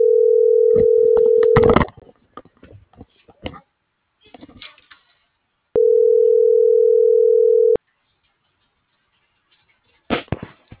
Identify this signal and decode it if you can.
Ringback tone (North America), 440 + 480 Hz.
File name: CENTRAL BLN / out marco.llanos